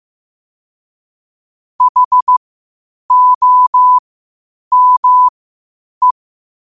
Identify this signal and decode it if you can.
Morse code, heard as a keyed tone, about 15 wpm, spaced as for 8 wpm, 1000 Hz. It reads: HOME
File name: qotdfiles / 8wpm